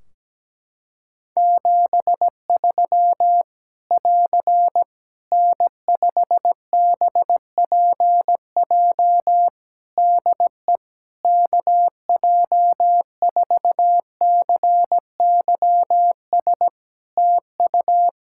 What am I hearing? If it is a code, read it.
Morse code, heard as a keyed tone, about 17 wpm, 700 Hz. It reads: 73 <AR> N5BPJ DE KJ4CYS TU
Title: Practice QSO - Exchange 4 - Ending - 17wpm